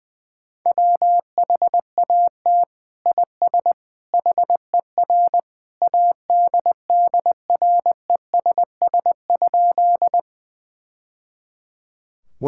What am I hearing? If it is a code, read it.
Morse code, heard as a keyed tone, about 20 wpm, 700 Hz. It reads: WHAT IS HER ADDRESS?
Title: Sentences from Top 1000 Words - Review Part 6 - 20wpm